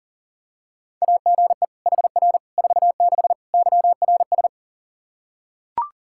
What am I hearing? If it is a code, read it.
Morse code, heard as a keyed tone, about 40 wpm, 700 Hz. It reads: AGE HR 46 YRS
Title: Practice QSO - Exchange 3 - More Info - 40wpm